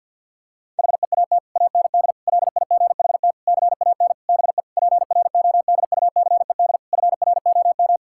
Morse code, heard as a keyed tone, about 52 wpm, 700 Hz. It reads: HEAT AND LIGHT CAN BE PRODUCED FROM
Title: Sentences from Top 1000 Words - Review Part 6 - 50wpm